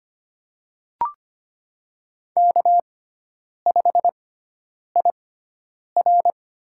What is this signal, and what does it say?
Morse code, heard as a keyed tone, about 25 wpm, spaced as for 9 wpm, 700 Hz. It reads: K5IR
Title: USA Call Signs - 6x Character Spacing - 25wpm